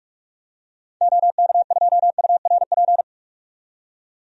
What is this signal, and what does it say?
Morse code, heard as a keyed tone, about 46 wpm, 700 Hz. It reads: OK2URP
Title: DXCC-100 Call Signs - 6 Characters - 45wpm